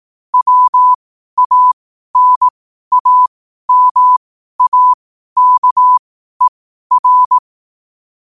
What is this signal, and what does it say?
Morse code, heard as a keyed tone, about 18 wpm, spaced as for 12 wpm, 1000 Hz. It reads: WANAMAKER
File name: qotdfiles / 12wpm